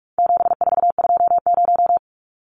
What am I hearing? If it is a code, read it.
Morse code, heard as a keyed tone, about 44 wpm, 700 Hz. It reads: 7420
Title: Numbers - 4-Digit 45wpm